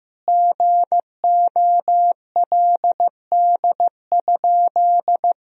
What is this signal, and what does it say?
Morse code, heard as a keyed tone, about 15 wpm, 700 Hz. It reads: GOLD?
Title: Sentences from Top 1000 Words - Review Part 3 - 15wpm